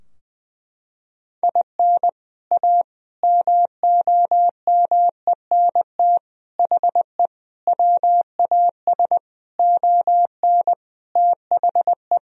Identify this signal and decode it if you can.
Morse code, heard as a keyed tone, about 20 wpm, 700 Hz. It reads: IN A MOMENT HE WAS ON THE
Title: Sentences from Top 1000 Words - Review Part 2 - 20wpm